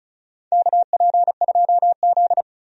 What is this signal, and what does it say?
Morse code, heard as a keyed tone, about 35 wpm, 700 Hz. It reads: KP2Z
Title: DXCC-100 Call Signs - 4 Characters - 35wpm